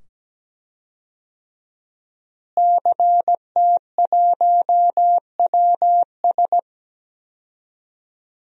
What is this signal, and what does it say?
Morse code, heard as a keyed tone, about 17 wpm, 700 Hz. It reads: CT1WS
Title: DXCC-100 Call Signs - 5 Characters - Repeated 2x - 17wpm